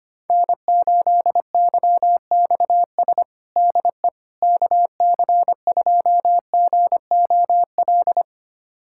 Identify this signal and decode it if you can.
Morse code, heard as a keyed tone, about 25 wpm, 700 Hz. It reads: N8YXS DE KC2GOL
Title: USA Callsign DE Callsign 25wpm